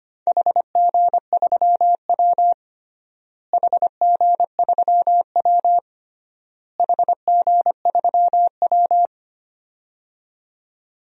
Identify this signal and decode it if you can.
Morse code, heard as a keyed tone, about 25 wpm, 700 Hz. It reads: HG3W HG3W HG3W
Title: DXCC-100 Call Signs - 4 Characters - Repeated 3x - 25wpm